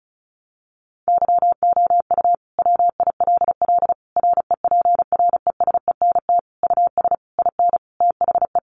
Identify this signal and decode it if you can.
Morse code, heard as a keyed tone, about 34 wpm, 700 Hz. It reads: YOU WILL REPRESENT US IN THE
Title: Sentences from Top 1200 Words 35wpm